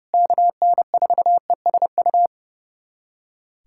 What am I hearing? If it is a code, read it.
Morse code, heard as a keyed tone, about 30 wpm, 700 Hz. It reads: KN4ESU
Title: USA Callsign DE Callsign 30wpm